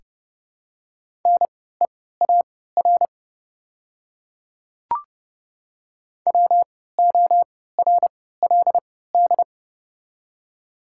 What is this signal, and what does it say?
Morse code, heard as a keyed tone, about 30 wpm, spaced as for 17 wpm, 700 Hz. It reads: NEAR WORLD
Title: 3x Character Spacing - Top 500 Words - 30wpm